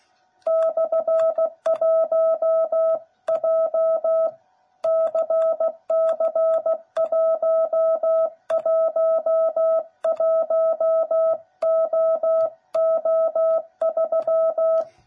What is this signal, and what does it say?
Morse code, heard as a keyed tone, about 16 wpm, 660 Hz. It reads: /1 JCC111OO3
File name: jh8cbh / 2021cw / cw09b